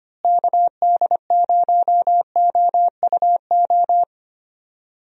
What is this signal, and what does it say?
Morse code, heard as a keyed tone, about 25 wpm, 700 Hz. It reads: KD0OUO
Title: USA Call Signs - 6 Characters - Repeated 3x - 25wpm